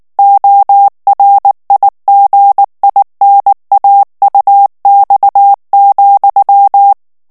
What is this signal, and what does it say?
Morse code, heard as a keyed tone, about 19 wpm, 800 Hz. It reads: ORIGINAUX,
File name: CW / FICHIERS-MP3 / FRANCAIS TEXTE CLAIR 2 VITESSE 18 MOTS MINUTES